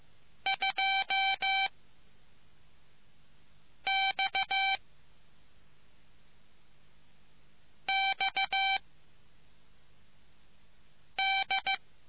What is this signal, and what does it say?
Morse code, heard as a keyed tone, about 15 wpm, 780 Hz. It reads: 2 X X D